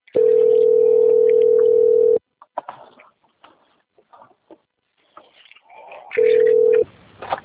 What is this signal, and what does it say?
Ringback tone (North America), 440 + 480 Hz.